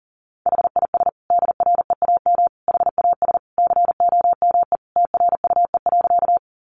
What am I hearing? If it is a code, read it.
Morse code, heard as a keyed tone, about 40 wpm, 700 Hz. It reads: HIS DREAM HAS COME TRUE.
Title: Sentences from Top 1500 Words 40wpm